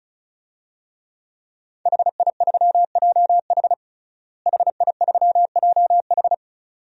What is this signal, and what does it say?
Morse code, heard as a keyed tone, about 34 wpm, 700 Hz. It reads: HI3JH HI3JH
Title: DXCC-100 Call Signs - 5 Characters - Repeated 2x - 35wpm